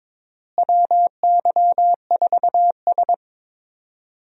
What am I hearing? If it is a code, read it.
Morse code, heard as a keyed tone, about 22 wpm, 700 Hz. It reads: WY4S
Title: USA Call Signs - 4 Characters - Repeated 2x - 22wpm